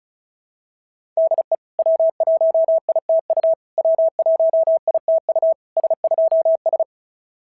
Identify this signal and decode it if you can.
Morse code, heard as a keyed tone, about 36 wpm, 690 Hz. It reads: DE W1ITU W1ITU S2S
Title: SOTA QSO Practice 35wpm